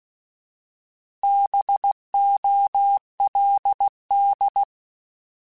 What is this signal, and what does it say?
Morse code, heard as a keyed tone, about 16 wpm, 800 Hz. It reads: BOLD
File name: Alphabetical / Alphabetical mp3s / Bs @15